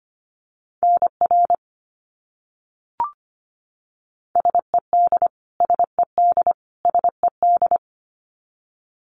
Morse code, heard as a keyed tone, about 25 wpm, 700 Hz. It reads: NR SED SED SED
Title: Abbreviations Repeated 3x - 25wpm